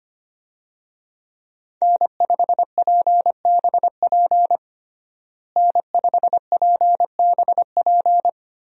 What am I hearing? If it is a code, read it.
Morse code, heard as a keyed tone, about 25 wpm, 700 Hz. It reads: N5PBP N5PBP
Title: USA Call Signs - 5 Characters - Repeated 3x - 25wpm